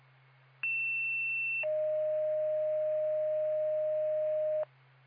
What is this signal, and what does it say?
Two-tone page, heard as a one-second tone, then a three-second tone, about 2700 then 620 Hz. It is heard from 0.6 to 4.6 s.